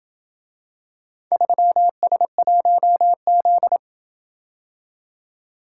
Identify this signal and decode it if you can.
Morse code, heard as a keyed tone, about 27 wpm, 700 Hz. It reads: 3S1Z